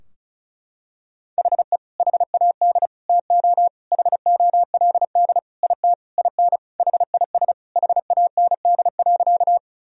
Morse code, heard as a keyed tone, about 36 wpm, 700 Hz. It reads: HE HAD TO HOLD IT IN HIS HAND.
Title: Sentences from Top 1000 Words - Review Part 2 - 35wpm